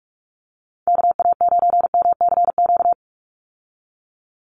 Morse code, heard as a keyed tone, about 44 wpm, 700 Hz. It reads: KA9MCQ
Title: USA Call Signs 45wpm